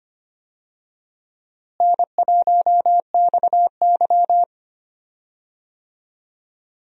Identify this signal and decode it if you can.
Morse code, heard as a keyed tone, about 25 wpm, 700 Hz. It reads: N1XY
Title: USA Call Signs - 4 Characters - Repeated 3x - 25wpm